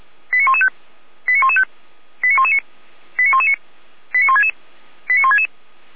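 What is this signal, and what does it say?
Five-tone selective call (ZVEI), heard as quick runs of five tones, tones of about 70 ms. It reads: 89115 89115 89119 89119 89166 89166
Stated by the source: ZVEI five-tone call 89097 received by SDR.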